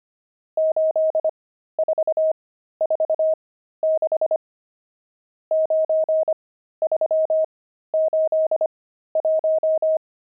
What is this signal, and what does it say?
Morse code, heard as a keyed tone, about 25 wpm, spaced as for 13 wpm, 630 Hz. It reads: 8446 9381
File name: pangram-fs-13-cs-25-nums14